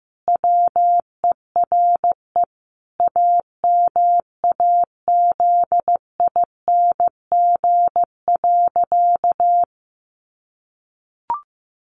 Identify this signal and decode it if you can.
Morse code, heard as a keyed tone, about 15 wpm, 700 Hz. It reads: WERE AMAZING.
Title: Sentences from Top 1500 Words 15wpm